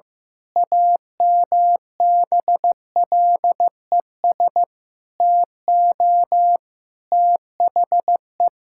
Morse code, heard as a keyed tone, about 16 wpm, 700 Hz. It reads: AMBLES TO THE
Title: Pangram Challenge v2 - 15wpm